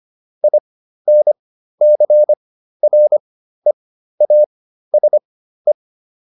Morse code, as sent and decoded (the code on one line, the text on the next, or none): .. -. -.-. .-. .- ... .
INCREASE